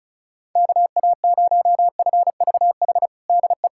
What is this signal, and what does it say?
Morse code, heard as a keyed tone, about 36 wpm, 700 Hz. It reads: KA0FVH DE
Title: USA Callsign DE Callsign - 6 Characters - 35wpm